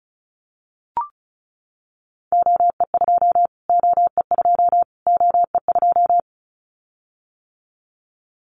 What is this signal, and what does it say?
Morse code, heard as a keyed tone, about 34 wpm, 700 Hz. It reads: OE2 OE2 OE2
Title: Callsign Prefixes for Top 25 DXCC - Repeated 3x - 35wpm